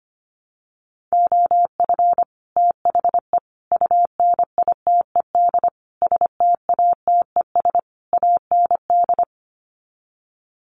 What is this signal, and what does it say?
Morse code, heard as a keyed tone, about 25 wpm, 700 Hz. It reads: OF THE UNITED STATES AND
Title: Sets of 5 Words 25wpm